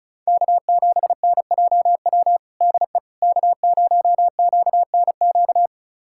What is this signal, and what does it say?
Morse code, heard as a keyed tone, about 36 wpm, 700 Hz. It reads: K7NJW DE K0QNQ
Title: USA Callsign DE Callsign - 5 Characters - 35wpm